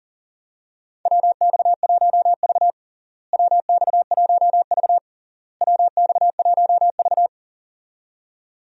Morse code, heard as a keyed tone, about 40 wpm, 700 Hz. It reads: WX1V WX1V WX1V
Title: USA Call Signs - 4 Characters - Repeated 3x - 40wpm